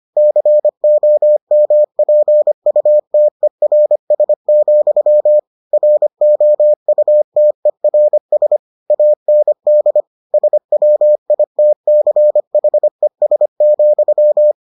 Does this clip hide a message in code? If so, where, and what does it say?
Yes: Morse code from 0.2 to 14.5 s, heard as a keyed tone, about 25 wpm, 600 Hz. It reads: COMPUTERS, ROUTERS AND SWITCHES,